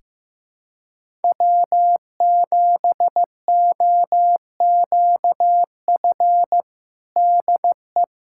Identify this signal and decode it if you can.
Morse code, heard as a keyed tone, about 15 wpm, 700 Hz. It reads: W7OQF DE
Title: USA Callsign DE Callsign - 5 Characters - 15wpm